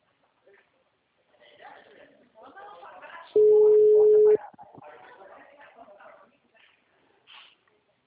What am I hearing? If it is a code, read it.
Ringback tone (Europe, 425 Hz), with speech.
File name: out- -2001- - - .7200